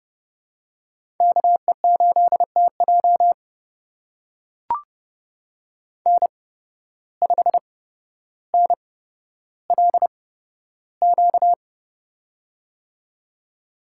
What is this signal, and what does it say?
Morse code, heard as a keyed tone, about 30 wpm, spaced as for 13 wpm, 700 Hz. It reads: KE8TJ N5NLQ
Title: USA Call Signs - 5 Characters - 8x Character Spacing - 30wpm